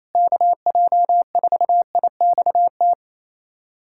Morse code, heard as a keyed tone, about 28 wpm, 700 Hz. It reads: KJ4IXT
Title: USA Callsign DE Callsign - 6 Characters - 28wpm